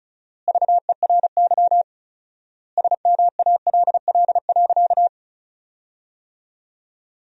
Morse code, heard as a keyed tone, about 34 wpm, 700 Hz. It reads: VERY SMALL.
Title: 4x Word-Spacing - Sentences from Top 400 Words - 35wpm